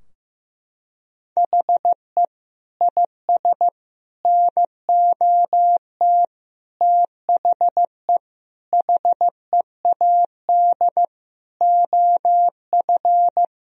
Morse code, heard as a keyed tone, about 15 wpm, 700 Hz. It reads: HE IS NOT THE HEAD OF
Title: Sentences from Top 1000 Words - Review Part 2 - 15wpm